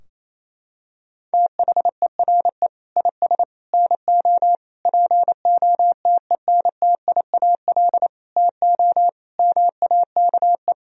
Morse code, heard as a keyed tone, about 28 wpm, 700 Hz. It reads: THERE IS NO POTENTIAL TO MAKE
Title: Sentences from Top 1000 Words - Review Part 4 - 28wpm